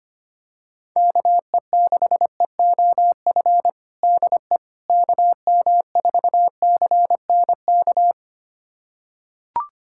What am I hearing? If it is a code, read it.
Morse code, heard as a keyed tone, about 25 wpm, 700 Hz. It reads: KE6EOF DE KM4CNK
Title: USA Callsign DE Callsign - 6 Characters - 25wpm